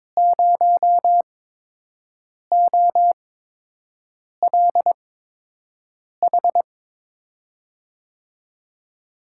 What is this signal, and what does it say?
Morse code, heard as a keyed tone, about 22 wpm, spaced as for 6 wpm, 700 Hz. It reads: 0OLH
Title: USA Call Signs - 6 Characters - 8x Character Spacing - 22wpm